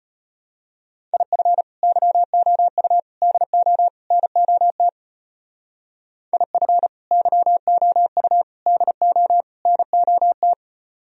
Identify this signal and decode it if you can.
Morse code, heard as a keyed tone, about 33 wpm, 700 Hz. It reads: IF YOU DO NOT IF YOU DO NOT